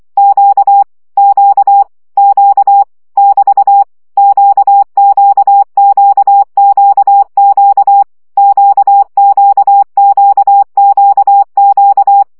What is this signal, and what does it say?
Morse code, heard as a keyed tone, about 24 wpm, 800 Hz. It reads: Q Q Q <BT> QQQQQ QQQQQ
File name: CW gyakorlas q 20 baud